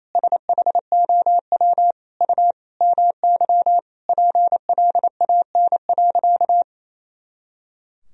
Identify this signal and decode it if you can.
Morse code, heard as a keyed tone, about 28 wpm, 700 Hz. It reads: SHOW U MY PLAN.
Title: Sentences from Top 500 Words and Abbreviations 28wpm